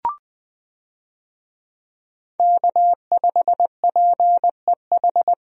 Morse code, heard as a keyed tone, about 20 wpm, 700 Hz. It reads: K5PEH
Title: USA Call Signs - 5 Characters - 20wpm